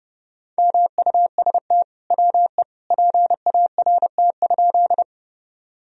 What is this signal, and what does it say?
Morse code, heard as a keyed tone, about 30 wpm, 700 Hz. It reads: MUST WE PART?
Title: Sentences from Top 1000 Words - Review Part 1 - 30wpm